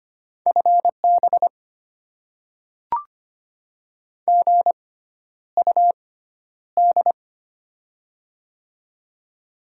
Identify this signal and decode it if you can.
Morse code, heard as a keyed tone, about 25 wpm, spaced as for 11 wpm, 700 Hz. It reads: FB GUD
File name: Top 20 QSO Elements - 6x Character Spacing - 25wpm